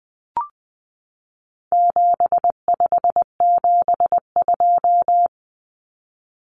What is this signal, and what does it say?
Morse code, heard as a keyed tone, about 20 wpm, 700 Hz. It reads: 7572
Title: Numbers - 4-Digit 20wpm